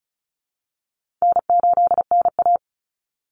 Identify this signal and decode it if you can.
Morse code, heard as a keyed tone, about 34 wpm, 700 Hz. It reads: N8NA
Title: Call Signs Repeated 2x - 35wpm